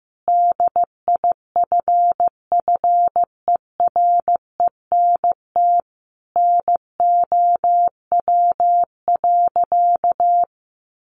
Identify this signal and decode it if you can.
Morse code, heard as a keyed tone, about 15 wpm, 700 Hz. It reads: DIFFERENT NOW.
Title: Sentences from Top 1200 Words 15wpm